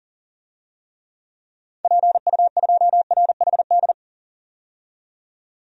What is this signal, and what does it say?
Morse code, heard as a keyed tone, about 40 wpm, 700 Hz. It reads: PU2RHD
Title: DXCC-100 Call Signs - 6 Characters - 40wpm